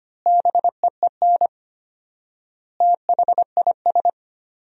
Morse code, heard as a keyed tone, about 25 wpm, 700 Hz. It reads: BEEN THIS
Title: 4x Word-Spacing - Sentences from Top 300 Words - 25wpm